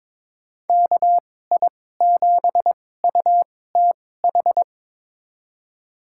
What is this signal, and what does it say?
Morse code, heard as a keyed tone, about 22 wpm, spaced as for 16 wpm, 700 Hz. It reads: KI7UTH
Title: USA Call Signs - 6 Characters - 2x Character Spacing - 22wpm